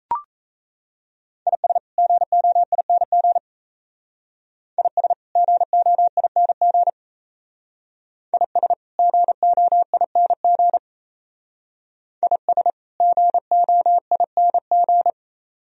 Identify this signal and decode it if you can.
Morse code, heard as a keyed tone, about 34 wpm, 700 Hz. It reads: IS GOING IS GOING IS GOING IS GOING
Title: Speed-Racing - Sets of 2 Words 28wpm